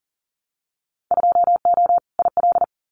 Morse code, heard as a keyed tone, about 39 wpm, 700 Hz. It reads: 2O IL
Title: Winter Field Day Exchanges 40wpm